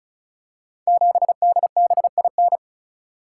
Morse code, heard as a keyed tone, about 36 wpm, 700 Hz. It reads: 7DBIN
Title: Code Groups - 5 Letter:Numbers - 35wpm